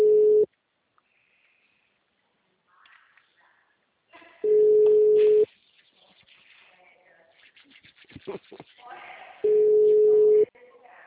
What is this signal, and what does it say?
Ringback tone (Europe, 425 Hz), with speech.
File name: out- -2002- - - .9172